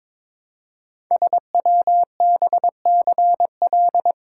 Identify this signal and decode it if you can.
Morse code, heard as a keyed tone, about 22 wpm, 700 Hz. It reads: SWBCL
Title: Code Groups - 5 Letters - 22wpm